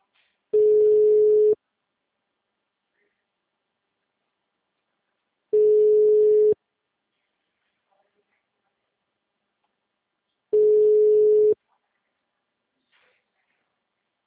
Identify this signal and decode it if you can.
Ringback tone (Europe, 425 Hz), with speech.
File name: out- -2014- - - .42696